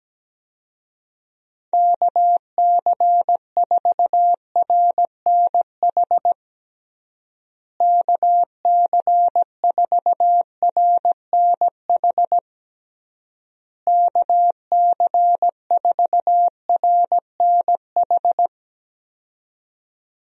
Morse code, heard as a keyed tone, about 17 wpm, 700 Hz. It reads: KC4RNH KC4RNH KC4RNH
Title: USA Call Signs - 6 Characters - Repeated 3x - 17wpm